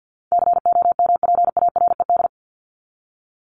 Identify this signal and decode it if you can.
Morse code, heard as a keyed tone, about 52 wpm, 700 Hz. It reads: COMPARED
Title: ED Suffix - Top 100 Words with Base Words 50wpm